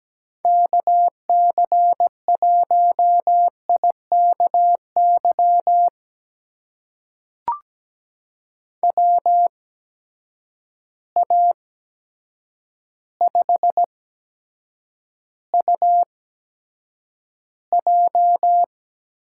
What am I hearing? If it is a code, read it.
Morse code, heard as a keyed tone, about 17 wpm, spaced as for 8 wpm, 700 Hz. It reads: KC1IKY WA5UJ
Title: USA Call Signs - 6 Characters - 8x Character Spacing - 17wpm